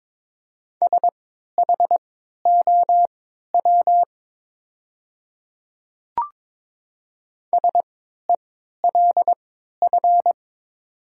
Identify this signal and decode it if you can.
Morse code, heard as a keyed tone, about 23 wpm, spaced as for 13 wpm, 700 Hz. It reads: SHOW SELF
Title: 3x Character Spacing - Top 500 Words - 22wpm